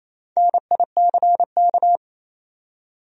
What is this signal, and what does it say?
Morse code, heard as a keyed tone, about 28 wpm, 700 Hz. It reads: NICK
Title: USA Common Names - Max 4 Letters - 28wpm